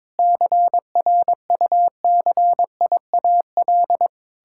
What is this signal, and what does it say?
Morse code, heard as a keyed tone, about 22 wpm, 700 Hz. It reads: CRUCIAL
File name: Sentences with 9-Word Phrases - 22wpm